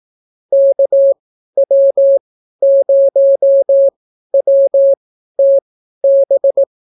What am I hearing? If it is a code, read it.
Morse code, heard as a keyed tone, about 18 wpm, spaced as for 12 wpm, 550 Hz. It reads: KW0WTB